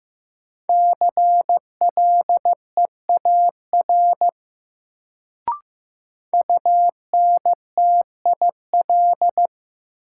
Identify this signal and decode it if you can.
Morse code, heard as a keyed tone, about 16 wpm, 700 Hz. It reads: CLEAR UNTIL
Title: Sets of 5 Words Encore - New Word Drill - 15wpm